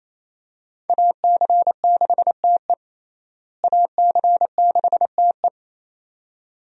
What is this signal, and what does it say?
Morse code, heard as a keyed tone, about 28 wpm, 700 Hz. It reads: AC6TE AC6TE
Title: USA Call Signs - 5 Characters - Repeated 2x - 28wpm